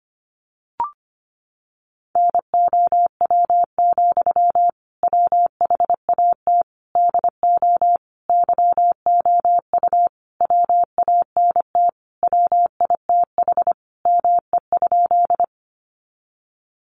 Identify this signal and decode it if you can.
Morse code, heard as a keyed tone, about 25 wpm, 700 Hz. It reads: NOW, WHAT DO YOU WANT WITH ME?
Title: Sentences from Top 200 Words 25wpm